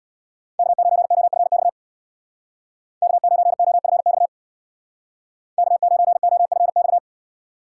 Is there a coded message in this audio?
Yes: Morse code from 0.6 to 7.0 s, heard as a keyed tone, about 65 wpm, 700 Hz. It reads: K9OFX K9OFX K9OFX